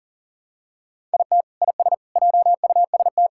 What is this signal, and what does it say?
Morse code, heard as a keyed tone, about 40 wpm, 700 Hz. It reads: IT IS JUST
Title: Sentences from Top 1000 Words - Review Part 6 - 40wpm